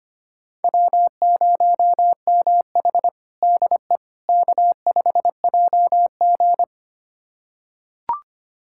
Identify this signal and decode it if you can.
Morse code, heard as a keyed tone, about 25 wpm, 700 Hz. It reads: W0MH DE K5JG
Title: USA Callsign DE Callsign - 4 Characters - 25wpm